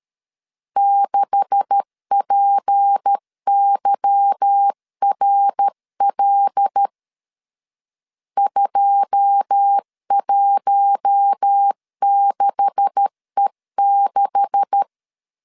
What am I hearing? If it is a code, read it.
Morse code, heard as a keyed tone, about 13 wpm, 800 Hz. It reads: 6PYRL 216E6
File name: LEZIONI / 9 286 12wpm 2305